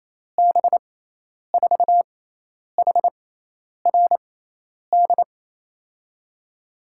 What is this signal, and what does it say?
Morse code, heard as a keyed tone, about 28 wpm, spaced as for 10 wpm, 700 Hz. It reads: B4HRD
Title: USA Call Signs - 6 Characters - 6x Character Spacing - 28wpm